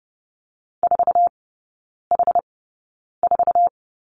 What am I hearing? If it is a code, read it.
Morse code, heard as a keyed tone, about 30 wpm, spaced as for 9 wpm, 700 Hz. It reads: 4H4
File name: 081 Lesson 14e - Number 2 - Three Characters 30wpm